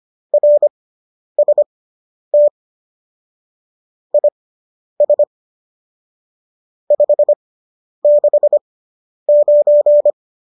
Morse code, heard as a keyed tone, about 25 wpm, spaced as for 10 wpm, 590 Hz. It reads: RST IS 569